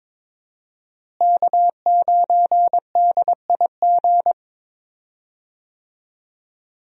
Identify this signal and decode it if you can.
Morse code, heard as a keyed tone, about 22 wpm, 700 Hz. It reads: K9DIG
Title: USA Call Signs - 5 Characters - Repeated 3x - 22wpm